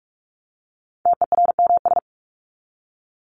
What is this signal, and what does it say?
Morse code, heard as a keyed tone, about 44 wpm, 700 Hz. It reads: TERMS
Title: ES:S Suffix - Top 100 Words with Base Words 45wpm